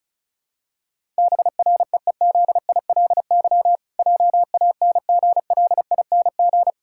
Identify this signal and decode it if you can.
Morse code, heard as a keyed tone, about 34 wpm, 700 Hz. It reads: BREEZILY JANGLING
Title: Pangram Challenge v2 - 35wpm